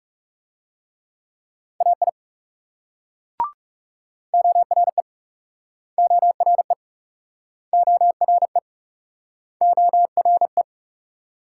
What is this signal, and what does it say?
Morse code, heard as a keyed tone, about 42 wpm, 700 Hz. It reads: AI ORE ORE ORE ORE